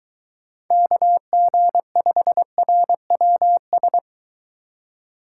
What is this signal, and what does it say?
Morse code, heard as a keyed tone, about 20 wpm, 700 Hz. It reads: KG5RWS